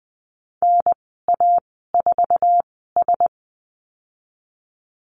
Morse code, heard as a keyed tone, about 19 wpm, spaced as for 14 wpm, 700 Hz. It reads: NA4S
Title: 055 Lesson 10g - Letter r - Call Signs 20wpm